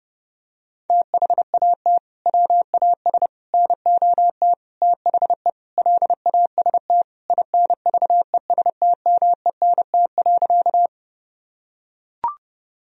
Morse code, heard as a keyed tone, about 30 wpm, 700 Hz. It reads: THAT WAS NOT THE LAST INVESTMENT.
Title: Sentences from Top 1000 Words - Review Part 4 - 30wpm